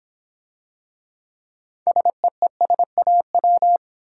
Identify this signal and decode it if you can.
Morse code, heard as a keyed tone, about 26 wpm, 700 Hz. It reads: SEESAW